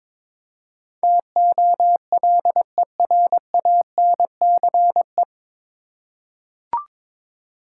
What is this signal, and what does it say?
Morse code, heard as a keyed tone, about 22 wpm, 700 Hz. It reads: TOLERANCE
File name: 9 Letter Words - First 3 Letters Given - 22wpm